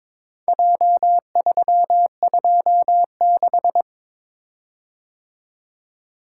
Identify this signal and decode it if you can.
Morse code, heard as a keyed tone, about 22 wpm, 700 Hz. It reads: J326
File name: practice / 22wpm / Four Letter-Number Combinations - 22wpm